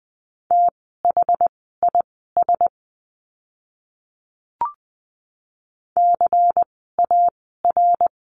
Morse code, heard as a keyed tone, about 20 wpm, spaced as for 15 wpm, 700 Hz. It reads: THIS CAR
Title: 108 Lesson 18f - Letter m - Words 20wpm